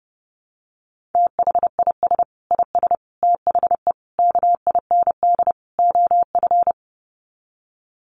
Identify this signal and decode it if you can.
Morse code, heard as a keyed tone, about 30 wpm, 700 Hz. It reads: THIS IS THE KIND OF
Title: Sets of 5 Words Encore 30wpm